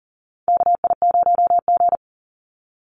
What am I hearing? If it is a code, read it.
Morse code, heard as a keyed tone, about 40 wpm, 700 Hz. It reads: KI0G
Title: NA SOTA Chaser Callsigns 40wpm